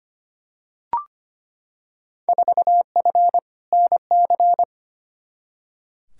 Morse code, heard as a keyed tone, about 25 wpm, 700 Hz. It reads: 4F NC
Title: Field Day Exchanges - 25wpm